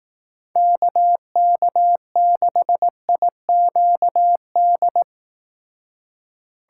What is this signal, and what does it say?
Morse code, heard as a keyed tone, about 18 wpm, 700 Hz. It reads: KK6IQD
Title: Speed Racing - USA 2x3 Call Signs 15wpm